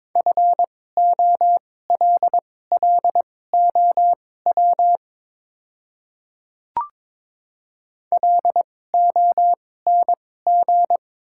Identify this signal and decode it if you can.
Morse code, heard as a keyed tone, about 22 wpm, spaced as for 16 wpm, 700 Hz. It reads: FOLLOW LONG
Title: 2x Character Spacing - Top 500 Words - 22wpm